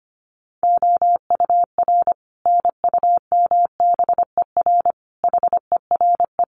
Morse code, heard as a keyed tone, about 25 wpm, 700 Hz. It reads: OUR NUMBER HERE
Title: Sets of 5 Words Encore2 25wpm